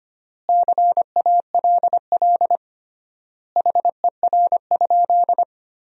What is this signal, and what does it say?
Morse code, heard as a keyed tone, about 25 wpm, 700 Hz. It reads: CALL HER?
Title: 3x Word-Spacing - Sentences from Top 100 Words - 25wpm